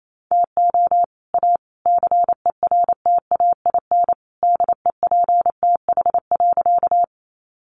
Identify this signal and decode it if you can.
Morse code, heard as a keyed tone, about 28 wpm, 700 Hz. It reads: TO A CERTAIN DEPTH.
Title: Sentences from Top 1200 Words 28wpm